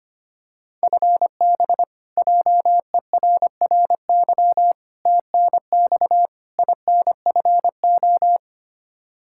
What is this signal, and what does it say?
Morse code, heard as a keyed tone, about 25 wpm, 700 Hz. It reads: FB JERRY TNX INFO
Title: K1USN SST QSO Practice 25wpm